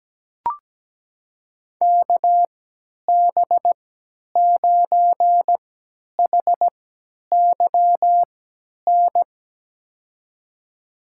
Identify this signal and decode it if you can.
Morse code, heard as a keyed tone, about 17 wpm, spaced as for 10 wpm, 700 Hz. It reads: KB9HYN
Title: USA Call Signs - 6 Characters - 3x Character Spacing - 17wpm